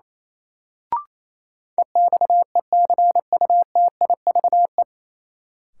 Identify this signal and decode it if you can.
Morse code, heard as a keyed tone, about 28 wpm, 700 Hz. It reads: EXECUTIVE
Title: Sets of 5 Words Encore - New Word Drill - 28wpm